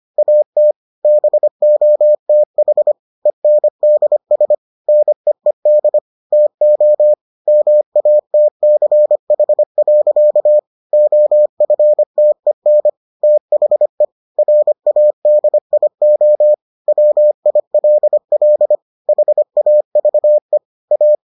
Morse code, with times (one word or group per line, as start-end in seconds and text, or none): 0.2-0.7 AT
1.1-2.9 BOTH
3.3-4.5 ENDS
4.9-6.0 NEED
6.3-7.1 TO
7.5-10.6 MATCH.
10.9-12.9 OFTEN
13.2-14.1 THE
14.4-16.5 RADIO
16.9-18.8 WILL
19.1-20.6 HAVE
20.9-21.1 A